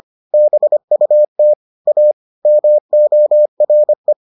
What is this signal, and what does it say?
Morse code, heard as a keyed tone, about 25 wpm, 600 Hz. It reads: BUT A MORE